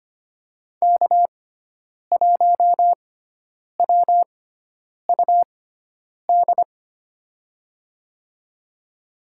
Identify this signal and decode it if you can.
Morse code, heard as a keyed tone, about 25 wpm, spaced as for 9 wpm, 700 Hz. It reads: K1WUD
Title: USA Call Signs - 5 Characters - 6x Character Spacing - 25wpm